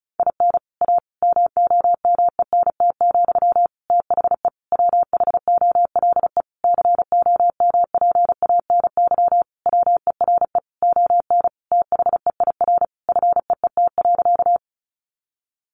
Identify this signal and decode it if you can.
Morse code, heard as a keyed tone, about 34 wpm, 700 Hz. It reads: IN A MOMENT, THE WHOLE COMPANY WERE ON THEIR FEET.
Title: Sentences from Top 1000 Words 35wpm